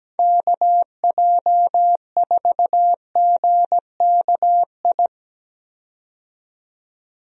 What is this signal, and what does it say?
Morse code, heard as a keyed tone, about 17 wpm, 700 Hz. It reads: KJ4GKI
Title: USA Call Signs - 6 Characters - 17wpm